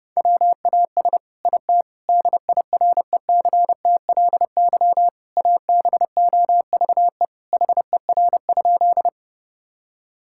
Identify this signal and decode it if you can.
Morse code, heard as a keyed tone, about 32 wpm, 700 Hz. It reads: WAS IT DIRECTLY ABOVE HER?
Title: Sentences from Top 1000 Words - Review Part 5 - 30wpm